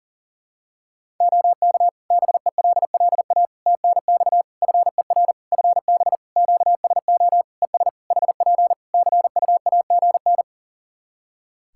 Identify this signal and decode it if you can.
Morse code, heard as a keyed tone, about 37 wpm, 700 Hz. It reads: OK BELLA TNX FER FB QSO ES HP CUAGN
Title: K1USN SST QSO Practice 40wpm